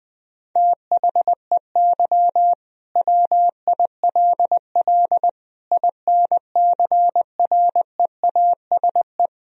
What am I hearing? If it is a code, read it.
Morse code, heard as a keyed tone, about 20 wpm, 700 Hz. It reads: THEY WILL INCREASE
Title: Sentences from Top 1000 Words - Review Part 4 - 20wpm